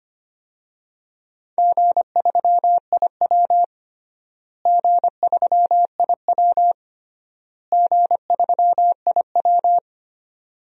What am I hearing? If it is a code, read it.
Morse code, heard as a keyed tone, about 25 wpm, 700 Hz. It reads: G3IW G3IW G3IW
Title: DXCC-100 Call Signs - 4 Characters - Repeated 3x - 25wpm